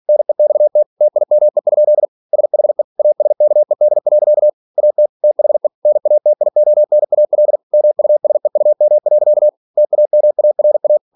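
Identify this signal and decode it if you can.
Morse code, heard as a keyed tone, about 46 wpm, 600 Hz. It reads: NEXT TIME? SHE ASKED. AT THE NATIONAL MUSEUM. TAMARA
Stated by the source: Practice text from an English novel sent at 47 WPM.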